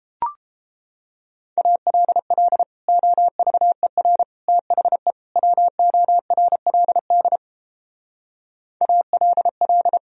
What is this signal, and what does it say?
Morse code, heard as a keyed tone, about 31 wpm, 700 Hz. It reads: ALL OVER THE WORLD ALL